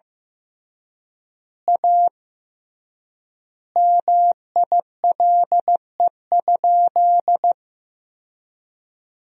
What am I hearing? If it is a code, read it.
Morse code, heard as a keyed tone, about 15 wpm, 700 Hz. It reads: A MILE?
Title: 3x Word-Spacing - Sentences from Top 400 Words - 15wpm